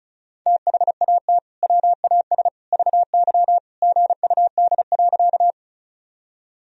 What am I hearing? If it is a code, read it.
Morse code, heard as a keyed tone, about 36 wpm, 700 Hz. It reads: THAT WAS VY GUD.
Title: Sentences from Top 200 Words and Abbreviations 35wpm